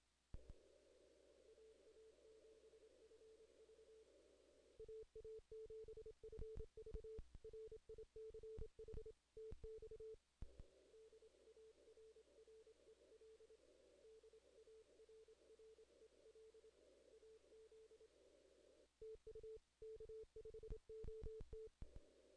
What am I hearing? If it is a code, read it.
Morse code, heard as a keyed tone, about 27 wpm, 440 Hz. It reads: AA7FV AA7FV RICH TX DARREL DARREL AZ TU K5OT
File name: 2016 JAN CW NAQP / 2016 Jan naqp k5ot - 0-11